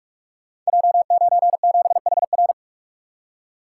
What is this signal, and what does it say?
Morse code, heard as a keyed tone, about 44 wpm, 690 Hz. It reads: J97HR
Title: Code Groups - 5 Letter:Numbers - 45wpm